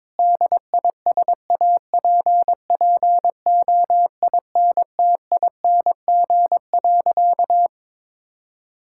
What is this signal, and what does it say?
Morse code, heard as a keyed tone, about 22 wpm, 700 Hz. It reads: DISAPPOINTING.
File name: practice / 22wpm / Sentences with 4-Word Phrases - 22wpm